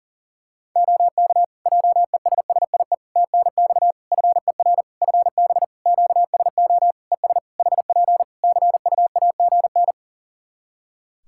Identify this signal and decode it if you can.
Morse code, heard as a keyed tone, about 40 wpm, 700 Hz. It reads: OK JESSIE TNX FER FB QSO ES HP CUAGN
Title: Practice QSO - Exchange 4 - Ending - 40wpm